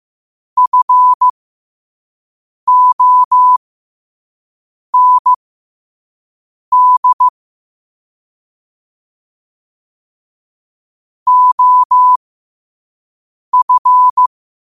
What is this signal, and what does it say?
Morse code, heard as a keyed tone, about 15 wpm, spaced as for 5 wpm, 1000 Hz. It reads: FOND OF
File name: qotdfiles / 5wpm